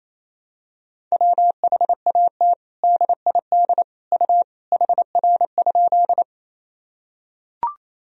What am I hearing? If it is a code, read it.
Morse code, heard as a keyed tone, about 28 wpm, 700 Hz. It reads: WHAT DID U HR?
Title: Sentences from Top 300 Words and Abbreviations 28wpm